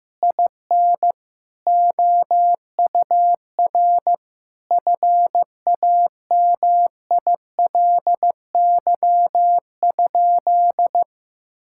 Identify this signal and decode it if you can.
Morse code, heard as a keyed tone, about 15 wpm, 700 Hz. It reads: IN OUR FAMILY?
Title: Sentences from Top 1000 Words - Review Part 6 - 15wpm